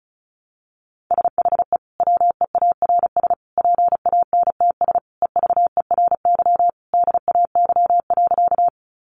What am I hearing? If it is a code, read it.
Morse code, heard as a keyed tone, about 34 wpm, 700 Hz. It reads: SHE WEARS PANTS EVERY DAY.
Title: Additional Easy Phrases 35wpm